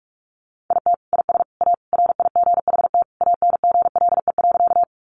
Morse code, heard as a keyed tone, about 44 wpm, 700 Hz. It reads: IT IS A RIGHT ANGLE.
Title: Sentences from Top 1500 Words 45wpm